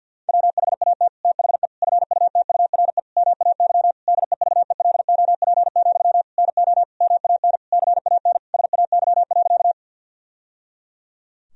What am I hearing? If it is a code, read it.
Morse code, heard as a keyed tone, about 47 wpm, 700 Hz. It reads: WHAT THE FUTURE MAY DEVELOP, NO MAN CAN SAY.
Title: Sentences from Top 1000 Words - Review Part 2 - 50wpm